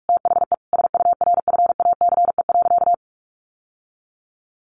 Morse code, heard as a keyed tone, about 44 wpm, 700 Hz. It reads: THE SURFACE.
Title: Sentences from Top 600 Words 45wpm